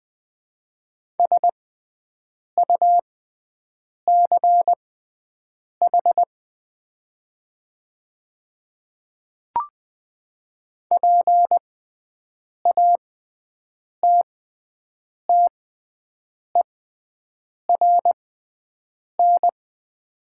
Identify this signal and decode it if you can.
Morse code, heard as a keyed tone, about 20 wpm, spaced as for 7 wpm, 700 Hz. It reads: SUCH PATTERN
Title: 6x Character Spacing - Top 500 Words - 20wpm